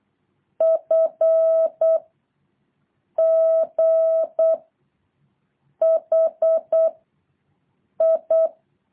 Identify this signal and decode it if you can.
Morse code, heard as a keyed tone, about 8 wpm, 660 Hz. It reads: F G H I
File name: jh8cbh / 2023cw